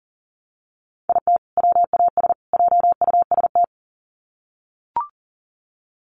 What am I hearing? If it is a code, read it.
Morse code, heard as a keyed tone, about 40 wpm, 700 Hz. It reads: IT WAS JUST
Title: Sets of 3 Words 40wpm